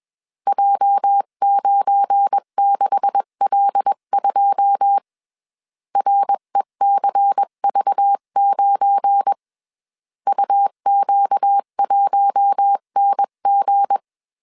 Morse code, heard as a keyed tone, about 21 wpm, 800 Hz. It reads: J96L2 RECV9 UQ1NG
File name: CW / DET / 1 9 1500C S B 20wpm 2313